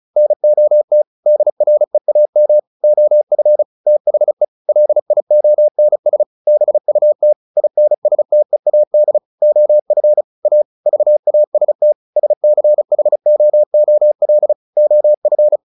Morse code, heard as a keyed tone, about 34 wpm, 600 Hz. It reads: NOT DREAM OF THE LIONS BUT INSTEAD OF A VAST SCHOOL OF
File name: the old man and the sea 2 35wpm